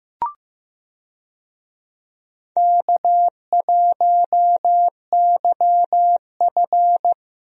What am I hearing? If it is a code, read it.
Morse code, heard as a keyed tone, about 16 wpm, 700 Hz. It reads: K1YF
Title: USA Call Signs - 4 Characters - Repeated 3x - 15wpm